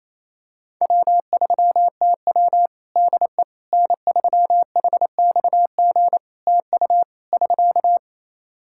Morse code, heard as a keyed tone, about 28 wpm, 700 Hz. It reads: W3TW DE N3HXG TU <SK>
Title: Practice QSO - Exchange 4 - Ending - 28wpm